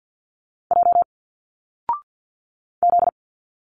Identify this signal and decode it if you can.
Morse code, heard as a keyed tone, about 50 wpm, 700 Hz. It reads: J Z
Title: Single Letters 50wpm